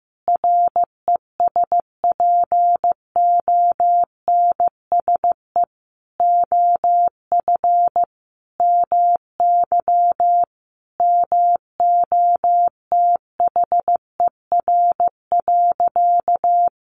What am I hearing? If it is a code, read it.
Morse code, heard as a keyed tone, about 15 wpm, 700 Hz. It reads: RESPONSE OF MY MOTHER.
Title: Sentences from Top 1400 Words 15wpm